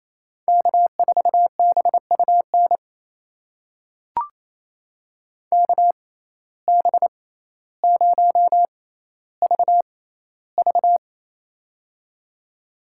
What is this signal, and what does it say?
Morse code, heard as a keyed tone, about 28 wpm, spaced as for 14 wpm, 700 Hz. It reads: K4BUN KB0VV
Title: USA Call Signs - 5 Characters - 6x Character Spacing - 28wpm